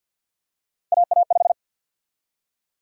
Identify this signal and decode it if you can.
Morse code, heard as a keyed tone, about 46 wpm, 700 Hz. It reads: AA5
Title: USA Call Sign Prefixes - 50wpm